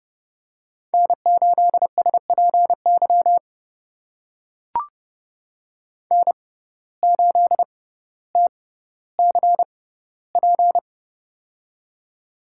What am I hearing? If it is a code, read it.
Morse code, heard as a keyed tone, about 30 wpm, spaced as for 15 wpm, 700 Hz. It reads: N8SPY N8TCP
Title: USA Call Signs - 5 Characters - 6x Character Spacing - 30wpm